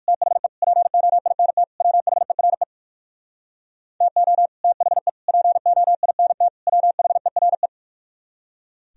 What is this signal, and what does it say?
Morse code, heard as a keyed tone, about 62 wpm, 700 Hz. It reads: THE POINT WHERE TO THE POINT WHERE